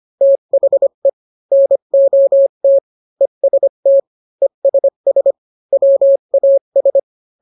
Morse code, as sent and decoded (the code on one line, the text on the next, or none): - .... / -. --- - / . ... - / . ... ... / .-- .- ...
THE NOT EST ESS WAS